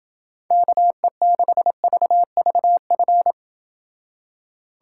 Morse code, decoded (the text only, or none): KE6VVF